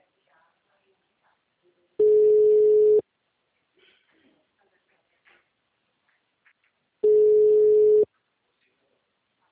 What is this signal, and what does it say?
Ringback tone (Europe, 425 Hz), with speech.